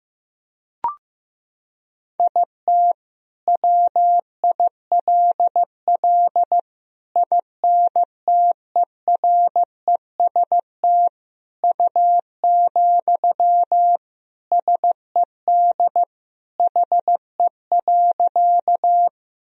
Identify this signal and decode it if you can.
Morse code, heard as a keyed tone, about 15 wpm, 700 Hz. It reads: IT WILL INTEREST U, SED HE.
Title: Sentences from Top 400 Words and Abbreviations 15wpm